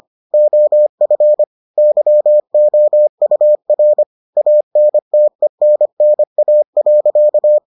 Morse code, heard as a keyed tone, about 25 wpm, 600 Hz. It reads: OF YOUR ANTENNA.